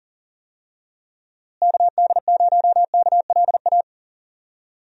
Morse code, heard as a keyed tone, about 39 wpm, 700 Hz. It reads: KD0KLA
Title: USA Call Signs - 6 Characters - Repeated 3x - 40wpm